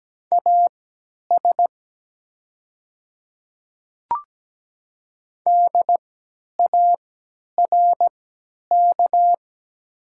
Morse code, heard as a keyed tone, about 17 wpm, spaced as for 10 wpm, 700 Hz. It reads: AS DARK
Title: 3x Character Spacing - Top 500 Words - 17wpm